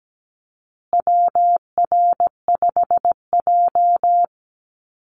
Morse code, heard as a keyed tone, about 17 wpm, 700 Hz. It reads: WR5J
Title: Call Signs Repeated 2x - 17wpm